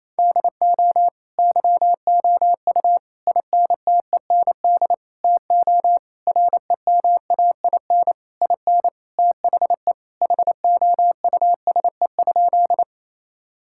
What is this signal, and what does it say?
Morse code, heard as a keyed tone, about 28 wpm, 700 Hz. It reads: DO YOU INTEND TO REMAIN IN THE HOUSE?
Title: Sentences from Top 1600 Words 28wpm